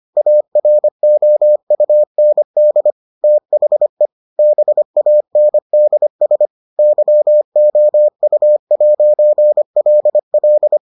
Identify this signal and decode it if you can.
Morse code, heard as a keyed tone, about 25 wpm, 600 Hz. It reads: AROUND THE BANDS YOU'LL